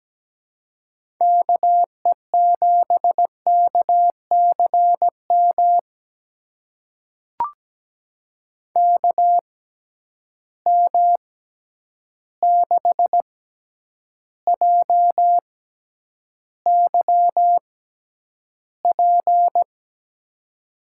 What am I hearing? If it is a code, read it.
Morse code, heard as a keyed tone, about 17 wpm, spaced as for 9 wpm, 700 Hz. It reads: KE7KCM KM6JYP
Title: USA Call Signs - 6 Characters - 6x Character Spacing - 17wpm